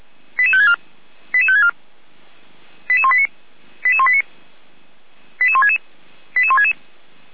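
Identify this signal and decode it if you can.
Five-tone selective call (ZVEI), heard as quick runs of five tones, tones of about 70 ms. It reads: 88564 88564 88189 88189 88166 88166
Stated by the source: ZVEI five-tone call received by SDR.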